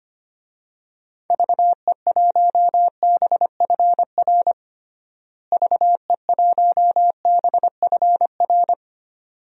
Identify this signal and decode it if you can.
Morse code, heard as a keyed tone, about 25 wpm, 700 Hz. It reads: VE1BFR VE1BFR
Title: DXCC-100 Call Signs - 6 Characters - Repeated 2x - 25wpm